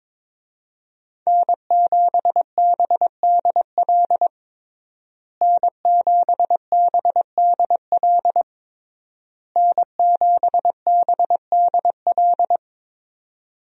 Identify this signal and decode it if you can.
Morse code, heard as a keyed tone, about 22 wpm, 700 Hz. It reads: N7BDL N7BDL N7BDL
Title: USA Call Signs - 5 Characters - Repeated 3x - 22wpm